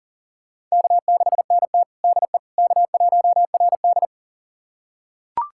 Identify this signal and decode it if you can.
Morse code, heard as a keyed tone, about 40 wpm, 700 Hz. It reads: K6NT DE K1RD
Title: USA Callsign DE Callsign - 4 Characters - 40wpm